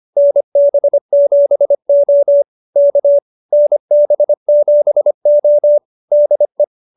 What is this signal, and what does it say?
Morse code, heard as a keyed tone, about 25 wpm, 590 Hz. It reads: NB7O K NB7O DE